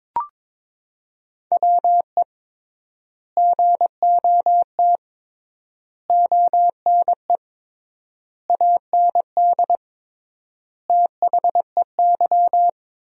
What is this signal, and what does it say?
Morse code, heard as a keyed tone, about 21 wpm, 700 Hz. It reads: WE GOT ONE AND THEY
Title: 3x Word-Spacing - Sentences from Top 300 Words - 22wpm